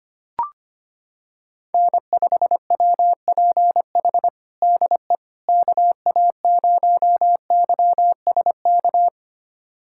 Morse code, heard as a keyed tone, about 25 wpm, 700 Hz. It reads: N5WPH DE KA0YSK
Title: USA Callsign DE Callsign 25wpm